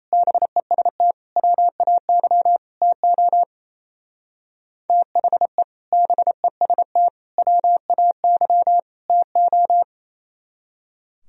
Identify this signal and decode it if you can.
Morse code, heard as a keyed tone, about 34 wpm, 700 Hz. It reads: BEST WAY TO THE BEST WAY TO
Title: Speed-Racing - Sets of 4 Words 28wpm